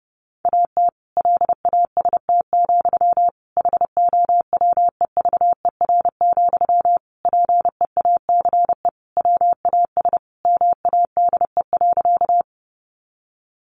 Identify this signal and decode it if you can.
Morse code, heard as a keyed tone, about 28 wpm, 700 Hz. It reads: AT LAST, HOWEVER, PEACE WAS MADE.
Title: Sentences from Top 1500 Words 30wpm